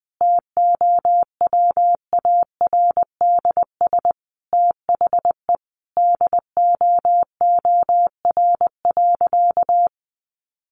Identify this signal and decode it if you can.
Morse code, heard as a keyed tone, about 19 wpm, 700 Hz. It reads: TOWARDS THE DOOR.
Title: Sentences from Top 1200 Words 20wpm